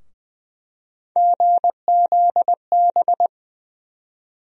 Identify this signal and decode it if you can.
Morse code, heard as a keyed tone, about 20 wpm, 700 Hz. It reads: GZB
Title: USA Call Sign Postfixes - 3 Letters 20wpm